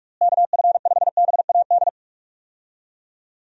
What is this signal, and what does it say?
Morse code, heard as a keyed tone, about 46 wpm, 700 Hz. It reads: KF5BAD
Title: USA Call Signs - 6 Characters - Repeated 2x - 45wpm